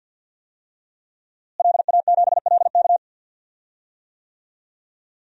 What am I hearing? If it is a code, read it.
Morse code, heard as a keyed tone, about 49 wpm, 700 Hz. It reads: LA7LK
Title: DXCC-100 Call Signs - 5 Characters - Repeated 2x - 50wpm